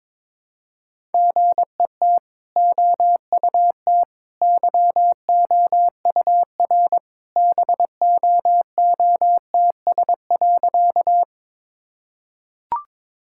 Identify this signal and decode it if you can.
Morse code, heard as a keyed tone, about 22 wpm, 700 Hz. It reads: GET OUT YOUR BOOTS.
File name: practice / 22wpm / Easy Sentences with Hints - 22wpm